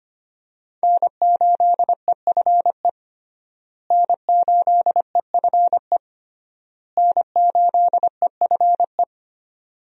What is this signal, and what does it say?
Morse code, heard as a keyed tone, about 25 wpm, 700 Hz. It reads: N8EFE N8EFE N8EFE
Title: USA Call Signs - 5 Characters - Repeated 3x - 25wpm